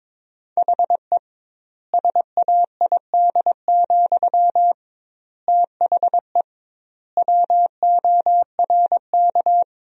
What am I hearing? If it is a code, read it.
Morse code, heard as a keyed tone, about 22 wpm, 700 Hz. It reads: HE SAID, THE WORK
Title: 2x Word-Spacing - Sentences from Top 400 Words - 22wpm